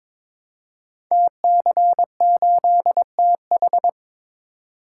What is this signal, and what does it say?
Morse code, heard as a keyed tone, about 23 wpm, 700 Hz. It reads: TC8TH
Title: Code Groups - 5 Letter:Numbers - 22wpm